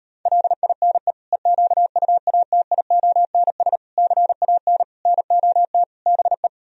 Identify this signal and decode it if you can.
Morse code, heard as a keyed tone, about 38 wpm, 700 Hz. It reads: LINE EQUATIONS CAN NOT BE